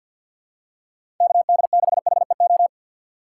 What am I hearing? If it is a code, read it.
Morse code, heard as a keyed tone, about 52 wpm, 700 Hz. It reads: KD6HEO
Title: USA Call Signs - 3x Character Spacing - 50wpm